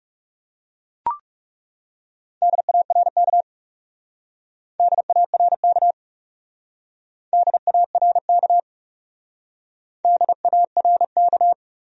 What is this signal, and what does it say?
Morse code, heard as a keyed tone, about 39 wpm, 700 Hz. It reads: DARK DARK DARK DARK